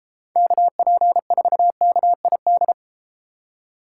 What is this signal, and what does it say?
Morse code, heard as a keyed tone, about 32 wpm, 700 Hz. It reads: KP4KID